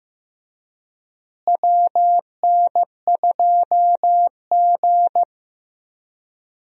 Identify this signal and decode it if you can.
Morse code, heard as a keyed tone, about 16 wpm, 700 Hz. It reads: WN2G
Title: USA Call Signs - 4 Characters - Repeated 2x - 15wpm